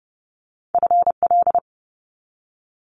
Morse code, heard as a keyed tone, about 30 wpm, 700 Hz. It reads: FL
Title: Multi-letter Phonemes 30wpm